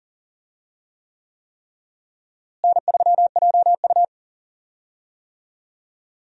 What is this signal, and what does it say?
Morse code, heard as a keyed tone, about 40 wpm, 700 Hz. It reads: N3JU
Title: USA Call Signs - 4 Characters - Repeated 3x - 40wpm